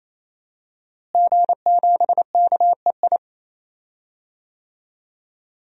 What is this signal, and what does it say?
Morse code, heard as a keyed tone, about 27 wpm, 700 Hz. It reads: G7KEI
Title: DXCC-100 Call Signs - 5 Characters - Repeated 3x - 28wpm